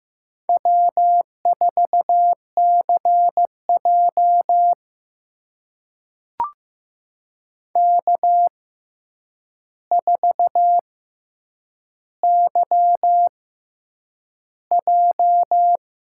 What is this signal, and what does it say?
Morse code, heard as a keyed tone, about 15 wpm, spaced as for 8 wpm, 700 Hz. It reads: W4CJ K4YJ
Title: USA Call Signs - 4 Characters - 6x Character Spacing - 15wpm